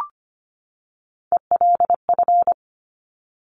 Morse code, heard as a keyed tone, about 25 wpm, 700 Hz. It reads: ELF
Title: Top Three Letter Combinations in Words 25wpm